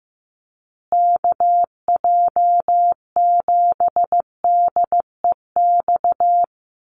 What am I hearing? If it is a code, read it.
Morse code, heard as a keyed tone, about 15 wpm, 700 Hz. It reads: KJ7DEX
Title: USA Call Signs Repeated 3x 15wpm